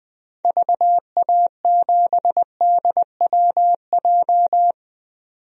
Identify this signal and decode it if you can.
Morse code, heard as a keyed tone, about 20 wpm, 700 Hz. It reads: VA7DWJ
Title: DXCC-100 Call Signs - 6 Characters - 20wpm